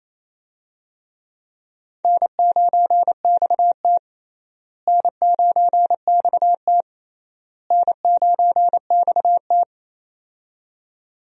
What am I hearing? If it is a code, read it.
Morse code, heard as a keyed tone, about 28 wpm, 700 Hz. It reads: N9XT N9XT N9XT
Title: USA Call Signs - 4 Characters - Repeated 3x - 28wpm